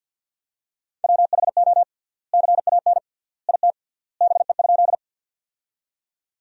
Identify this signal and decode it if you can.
Morse code, heard as a keyed tone, about 50 wpm, 700 Hz. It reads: WHO CAN IT BE?
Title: 3x Word-Spacing - Sentences from Top 100 Words - 50wpm